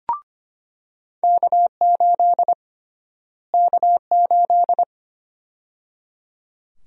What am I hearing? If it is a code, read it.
Morse code, heard as a keyed tone, about 25 wpm, 700 Hz. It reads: K8 K8
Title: USA Call Sign Prefixes - Repeated 2x - 25wpm